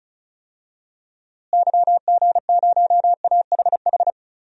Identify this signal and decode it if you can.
Morse code, heard as a keyed tone, about 36 wpm, 700 Hz. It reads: YG0AHH
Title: DXCC-100 Call Signs - 6 Characters - Repeated 3x - 35wpm